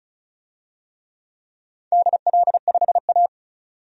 Morse code, heard as a keyed tone, about 36 wpm, 700 Hz. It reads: DL5A
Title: DXCC-100 Call Signs - 4 Characters - 35wpm